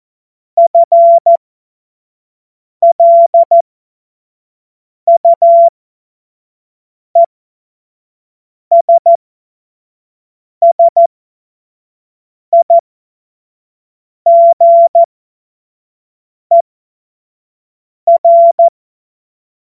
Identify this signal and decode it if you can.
Morse code, heard as a keyed tone, about 14 wpm, spaced as for 5 wpm, 680 Hz. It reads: FLUESSIGER
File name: Morsekurs / Witze / Witz 28. September 2021, 25 F70